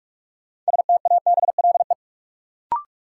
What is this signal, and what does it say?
Morse code, heard as a keyed tone, about 44 wpm, 700 Hz. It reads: STABLE
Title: Sentences from Top 1500 Words - New Word Drill - 45wpm